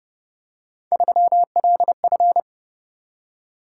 Morse code, heard as a keyed tone, about 29 wpm, 700 Hz. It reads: 3LF